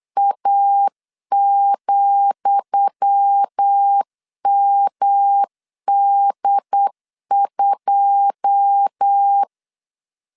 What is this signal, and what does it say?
Morse code, heard as a keyed tone, about 8 wpm, 800 Hz. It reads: A,MD2
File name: LEZIONI / 10 SIGNS / 10D COMA 8wpm 3749